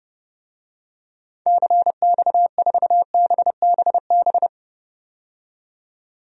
Morse code, heard as a keyed tone, about 30 wpm, 700 Hz. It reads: CX4BBB
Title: DXCC-100 Call Signs - 6 Characters - 30wpm